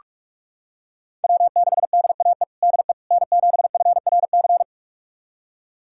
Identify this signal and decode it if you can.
Morse code, heard as a keyed tone, about 46 wpm, 700 Hz. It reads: W6DAE DE N7FRC
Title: USA Callsign DE Callsign - 5 Characters - 45wpm